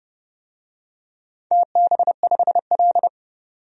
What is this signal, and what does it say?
Morse code, heard as a keyed tone, about 30 wpm, 700 Hz. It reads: TB5L
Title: DXCC-100 Call Signs - 4 Characters - 30wpm